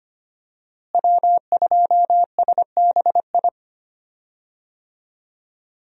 Morse code, heard as a keyed tone, about 25 wpm, 700 Hz. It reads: W2SBI
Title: USA Call Signs - 5 Characters - Repeated 3x - 25wpm